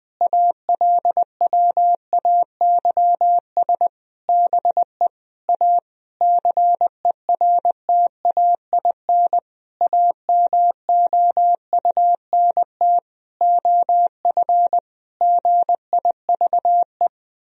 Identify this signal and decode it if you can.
Morse code, heard as a keyed tone, about 21 wpm, 700 Hz. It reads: ALWAYS BE A CERTAIN AMOUNT OF GIVE
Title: Sentences from Top 1000 Words - Review Part 4 - 20wpm